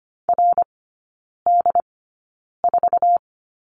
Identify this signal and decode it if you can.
Morse code, heard as a keyed tone, about 25 wpm, spaced as for 9 wpm, 700 Hz. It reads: RD4
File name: Lesson 15e - Number 5 - Three Characters 25wpm